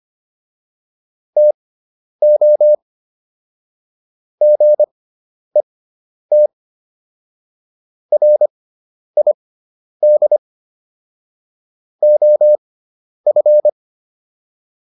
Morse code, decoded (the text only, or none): TO GET RID OF